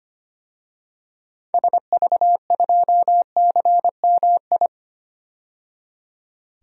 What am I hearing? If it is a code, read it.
Morse code, heard as a keyed tone, about 25 wpm, 700 Hz. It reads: SV2CMI
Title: DXCC-100 Call Signs - 6 Characters - 25wpm